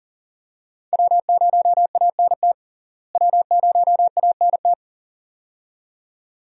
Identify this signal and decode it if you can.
Morse code, heard as a keyed tone, about 38 wpm, 700 Hz. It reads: W0ANT W0ANT
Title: USA Call Signs - 5 Characters - Repeated 2x - 40wpm